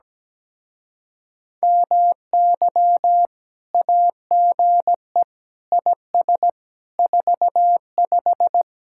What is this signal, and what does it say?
Morse code, heard as a keyed tone, about 18 wpm, 700 Hz. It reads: MY AGE IS 45
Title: Speed-Racing Common QSO Phrases - 17wpm